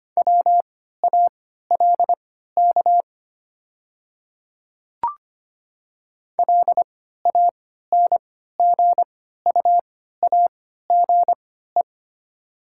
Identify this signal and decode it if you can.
Morse code, heard as a keyed tone, about 25 wpm, spaced as for 14 wpm, 700 Hz. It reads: WALK LANGUAGE
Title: 3x Character Spacing - Top 500 Words - 25wpm